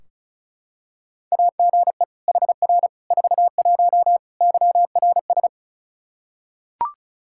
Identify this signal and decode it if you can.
Morse code, heard as a keyed tone, about 36 wpm, 700 Hz. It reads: AGE HR 41 YRS
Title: Practice QSO - Exchange 3 - More Info - 35wpm